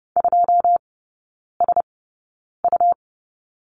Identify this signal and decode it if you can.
Morse code, heard as a keyed tone, about 30 wpm, spaced as for 9 wpm, 700 Hz. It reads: 2SU
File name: Lesson 17e - Letter u - Three Characters 30wpm